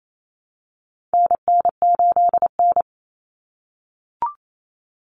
Morse code, heard as a keyed tone, about 28 wpm, 700 Hz. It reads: NN8N
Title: Common Cut Numbers - 4-Digit 28wpm